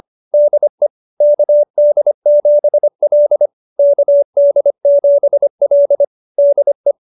Morse code, heard as a keyed tone, about 25 wpm, 570 Hz. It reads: DE KD7L KD7L DE